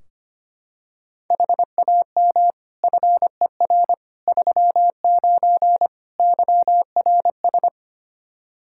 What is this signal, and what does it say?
Morse code, heard as a keyed tone, about 25 wpm, 700 Hz. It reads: HAM FER 39 YRS
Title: Practice QSO - Exchange 3 - More Info - 25wpm